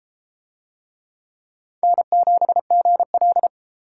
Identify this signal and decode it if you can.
Morse code, heard as a keyed tone, about 31 wpm, 700 Hz. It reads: N7GL